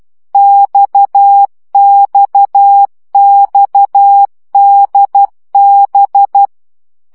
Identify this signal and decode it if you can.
Morse code, heard as a keyed tone, about 12 wpm, 800 Hz. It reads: XXXDB
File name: CW gyakorlas d b x 10 baud